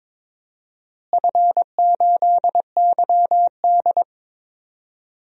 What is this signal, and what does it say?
Morse code, heard as a keyed tone, about 22 wpm, 700 Hz. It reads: F8YD
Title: DXCC-100 Call Signs - 4 Characters - Repeated 2x - 22wpm